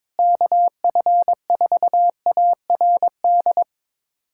Morse code, heard as a keyed tone, about 22 wpm, 700 Hz. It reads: KF4ARD
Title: USA Callsign DE Callsign - 6 Characters - 22wpm